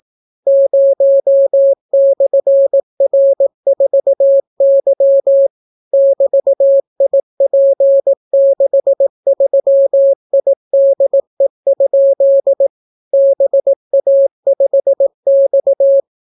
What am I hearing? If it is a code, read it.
Morse code, heard as a keyed tone, about 18 wpm, 550 Hz. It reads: 0/R4Y <BT>IP63IDE? BA5X